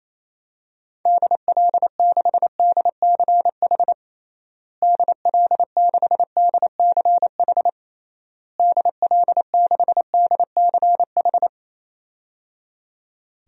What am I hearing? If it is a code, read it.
Morse code, heard as a keyed tone, about 28 wpm, 700 Hz. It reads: DL6DCH DL6DCH DL6DCH
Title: DXCC-100 Call Signs - 6 Characters - Repeated 3x - 28wpm